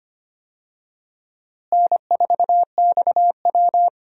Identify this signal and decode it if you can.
Morse code, heard as a keyed tone, about 25 wpm, 700 Hz. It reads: N4XW
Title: USA Call Signs - 4 Characters - Repeated 2x - 25wpm